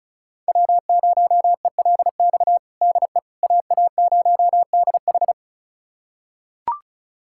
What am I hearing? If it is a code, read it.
Morse code, heard as a keyed tone, about 36 wpm, 700 Hz. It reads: W0ELX DE AA0DH
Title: USA Callsign DE Callsign - 5 Characters - 35wpm